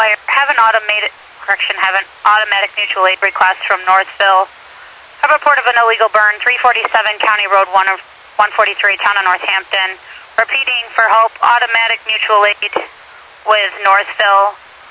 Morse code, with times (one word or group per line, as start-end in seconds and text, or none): none